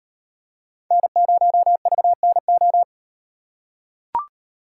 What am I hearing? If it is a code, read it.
Morse code, heard as a keyed tone, about 37 wpm, 700 Hz. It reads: N0VNO